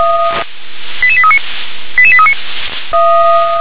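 Five-tone selective call (ZVEI), heard as quick runs of five tones, tones of about 70 ms. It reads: 88839 88839
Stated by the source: ZVEI five-tone selective call received by SDR.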